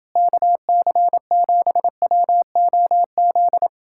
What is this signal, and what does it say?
Morse code, heard as a keyed tone, about 29 wpm, 700 Hz. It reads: KC7WOZ